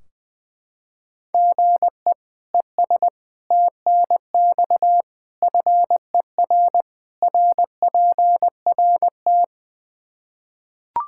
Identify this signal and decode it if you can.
Morse code, heard as a keyed tone, about 20 wpm, 700 Hz. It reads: GE ES TNX FER RPRT
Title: Practice QSO - Exchange 1 - Beginning - 20wpm